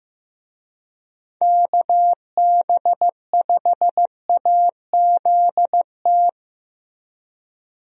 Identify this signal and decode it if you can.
Morse code, heard as a keyed tone, about 16 wpm, 700 Hz. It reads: KB5AZT
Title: USA Call Signs - 6 Characters - Repeated 3x - 15wpm